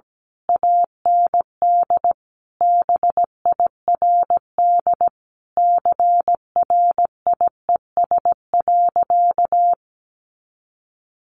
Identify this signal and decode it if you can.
Morse code, heard as a keyed tone, about 17 wpm, 700 Hz. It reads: AND BIRD CRIES.
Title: A Sound Of Thunder 17wpm